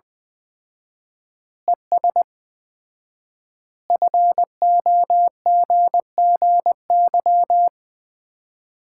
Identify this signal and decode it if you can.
Morse code, heard as a keyed tone, about 20 wpm, 700 Hz. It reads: ES FOGGY
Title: Common QSO Phrases - 4x Word-Spacing - 20wpm